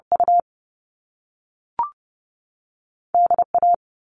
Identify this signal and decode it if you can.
Morse code, heard as a keyed tone, about 30 wpm, spaced as for 3 wpm, 700 Hz. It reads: UD A
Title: All Cut Numbers - 2-Digit 30wpm